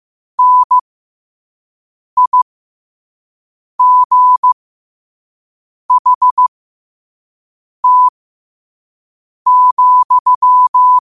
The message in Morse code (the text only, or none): NIGHT,